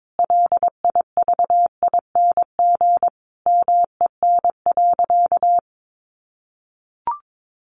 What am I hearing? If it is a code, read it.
Morse code, heard as a keyed tone, about 22 wpm, 700 Hz. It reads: LIVING MEN.
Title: Sentences from Top 1300 Words 22wpm